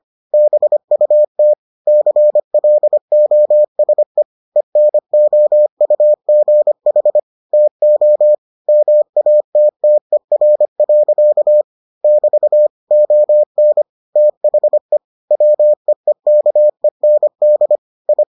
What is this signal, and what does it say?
Morse code, heard as a keyed tone, about 25 wpm, 600 Hz. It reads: BUT CLOSE ENOUGH TO MATTER. <BT> ON THE WEEKEND I